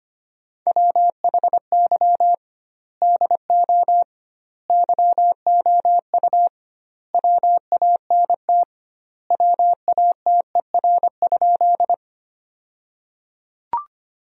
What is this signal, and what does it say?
Morse code, heard as a keyed tone, about 25 wpm, 700 Hz. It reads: WHY DO YOU WANT WATER?
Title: 2x Word-Spacing - Sentences from Top 200 Words - 25wpm